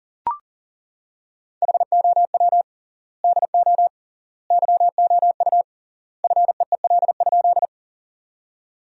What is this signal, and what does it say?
Morse code, heard as a keyed tone, about 37 wpm, 700 Hz. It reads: HOW DO YOU FEEL?
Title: 3x Word-Spacing - Sentences from Top 400 Words - 40wpm